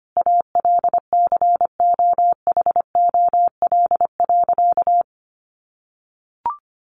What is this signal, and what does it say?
Morse code, heard as a keyed tone, about 25 wpm, 700 Hz. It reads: ALCOHOL.
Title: Sentences from Top 1400 Words 25wpm